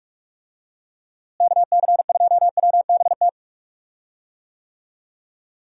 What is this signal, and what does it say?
Morse code, heard as a keyed tone, about 41 wpm, 700 Hz. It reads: KC2WBT
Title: USA Call Signs - 6 Characters - Repeated 3x - 45wpm